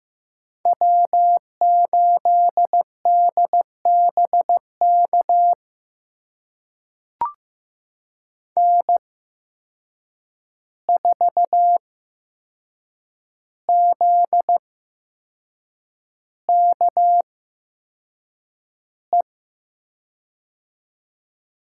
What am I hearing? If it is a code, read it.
Morse code, heard as a keyed tone, about 15 wpm, spaced as for 6 wpm, 700 Hz. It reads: W8DBK N4ZKE
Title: USA Call Signs - 5 Characters - 8x Character Spacing - 15wpm